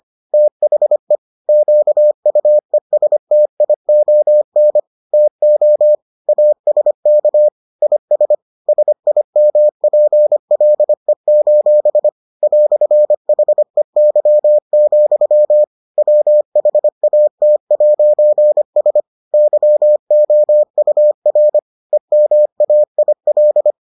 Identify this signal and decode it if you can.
Morse code, heard as a keyed tone, about 25 wpm, 600 Hz. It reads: THE QUESTION TO ASK IS SIMPLE: "HEY, WHAT'S YOUR EMAIL